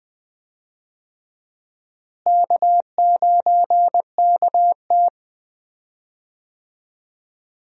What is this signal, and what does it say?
Morse code, heard as a keyed tone, about 20 wpm, 700 Hz. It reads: K9KT
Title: USA Call Signs - 4 Characters - Repeated 2x - 20wpm